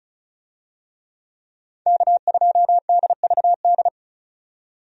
Morse code, heard as a keyed tone, about 36 wpm, 700 Hz. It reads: K2DVD
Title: USA Call Signs - 5 Characters - 35wpm